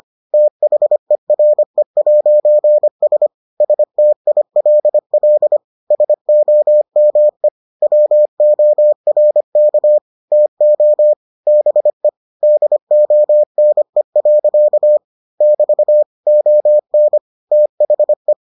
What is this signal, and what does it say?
Morse code, heard as a keyed tone, about 25 wpm, 600 Hz. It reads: THERE'S STILL SOME WORK TO BE DONE. <BT> ON THE